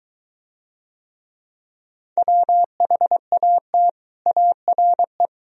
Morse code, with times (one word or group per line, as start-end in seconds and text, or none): 2.2-3.9 WHAT
4.3-5.3 ARE